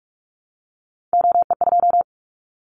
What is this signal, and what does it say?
Morse code, heard as a keyed tone, about 44 wpm, 700 Hz. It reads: OE2
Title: Callsign Prefixes for Top 100 DXCC - 45wpm